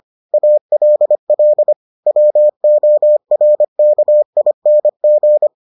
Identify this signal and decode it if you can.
Morse code, heard as a keyed tone, about 25 wpm, 600 Hz. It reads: ALL WORKING